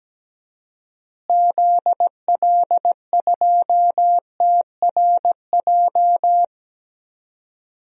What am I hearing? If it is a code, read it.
Morse code, heard as a keyed tone, about 17 wpm, 700 Hz. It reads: ZL2TRJ
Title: DXCC-100 Call Signs - 6 Characters - Repeated 2x - 17wpm